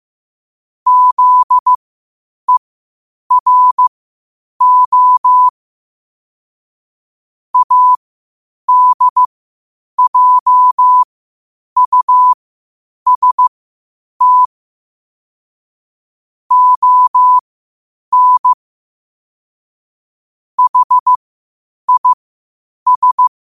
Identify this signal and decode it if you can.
Morse code, heard as a keyed tone, about 15 wpm, spaced as for 8 wpm, 1000 Hz. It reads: ZERO ADJUST ON HIS
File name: qotdfiles / 8wpm